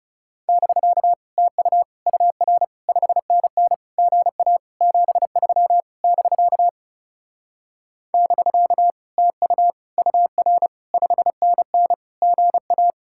Morse code, heard as a keyed tone, about 38 wpm, 700 Hz. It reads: <BK> TU UR 5NN GA 73 <BK> <BK> TU UR 5NN GA